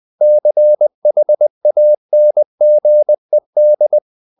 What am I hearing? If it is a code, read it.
Morse code, heard as a keyed tone, about 20 wpm, 600 Hz. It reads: CHANGED
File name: the old man and the sea part 2 20wpm